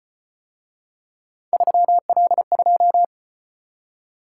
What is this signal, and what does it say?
Morse code, heard as a keyed tone, about 35 wpm, 700 Hz. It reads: 3L2